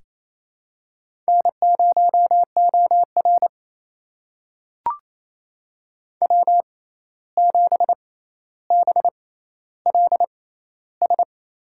Morse code, heard as a keyed tone, about 28 wpm, spaced as for 13 wpm, 700 Hz. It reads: N0OR W7BLS
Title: USA Call Signs - 6x Character Spacing - 28wpm